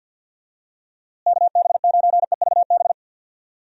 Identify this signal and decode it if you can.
Morse code, heard as a keyed tone, about 52 wpm, 700 Hz. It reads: KB9EVB
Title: USA Call Signs - 6 Characters - Repeated 3x - 50wpm